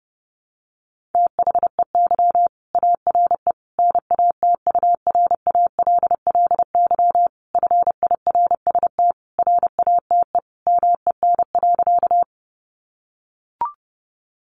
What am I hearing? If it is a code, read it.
Morse code, heard as a keyed tone, about 30 wpm, 700 Hz. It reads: THEY ARE NATURALLY FIRST RATE MEN.
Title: Sentences from Top 1300 Words 30wpm